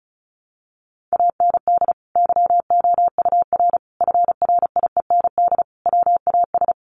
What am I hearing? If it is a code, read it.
Morse code, heard as a keyed tone, about 34 wpm, 700 Hz. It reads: AND YOUR FRIEND WAS
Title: Sentences from Top 1400 Words 35wpm